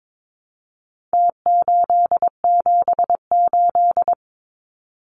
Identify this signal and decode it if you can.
Morse code, heard as a keyed tone, about 22 wpm, 700 Hz. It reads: T878
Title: Numbers - Contest Style - 4-Digit 22wpm